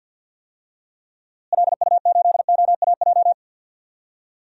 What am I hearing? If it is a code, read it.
Morse code, heard as a keyed tone, about 53 wpm, 700 Hz. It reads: LU8OAJ
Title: DXCC-100 Call Signs - 6 Characters - Repeated 2x - 50wpm